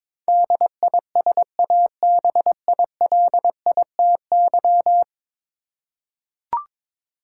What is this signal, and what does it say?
Morse code, heard as a keyed tone, about 22 wpm, 700 Hz. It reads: DISABILITY
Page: morse-code-ninja-practice-22wpm